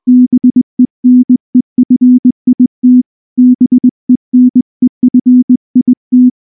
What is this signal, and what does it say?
Morse code, heard as a keyed tone, about 19 wpm, 250 Hz. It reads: BENEFIT BENEFIT
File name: morse / morse-training-20WPM-6